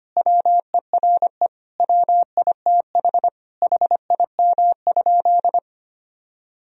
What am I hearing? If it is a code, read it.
Morse code, heard as a keyed tone, about 25 wpm, 700 Hz. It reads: WERE WITH HIM?
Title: Sentences from Top 1000 Words - Review Part 1 - 25wpm